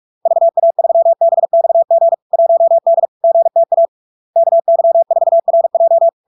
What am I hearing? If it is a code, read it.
Morse code, heard as a keyed tone, about 46 wpm, 670 Hz. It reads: VA3BXG 1D GTA KY4RJ
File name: LARK-FD-2022-Log-in-CW-45wpm-670-Hz